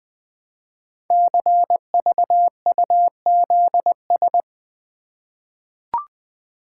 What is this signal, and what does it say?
Morse code, heard as a keyed tone, about 20 wpm, 700 Hz. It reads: CVUZS
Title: Code Groups - 5 Letters - 20wpm